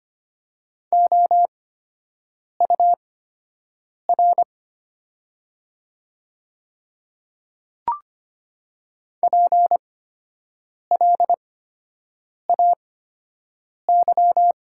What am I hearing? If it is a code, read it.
Morse code, heard as a keyed tone, about 25 wpm, spaced as for 7 wpm, 700 Hz. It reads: OUR PLAY
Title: 8x Character Spacing - Top 200 Words - 25wpm